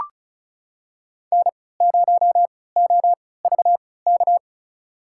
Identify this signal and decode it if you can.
Morse code, heard as a keyed tone, about 36 wpm, spaced as for 20 wpm, 700 Hz. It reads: N0OVK
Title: USA Call Signs - 3x Character Spacing - 35wpm